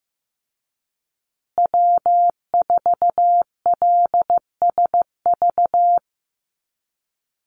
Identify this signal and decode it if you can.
Morse code, heard as a keyed tone, about 15 wpm, 700 Hz. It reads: W4LSV
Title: Call Signs Repeated 3x - 15wpm